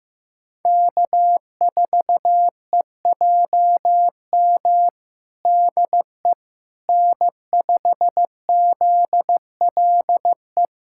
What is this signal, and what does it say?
Morse code, heard as a keyed tone, about 15 wpm, 700 Hz. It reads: K4EJM DE N5ZLE
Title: USA Callsign DE Callsign - 5 Characters - 15wpm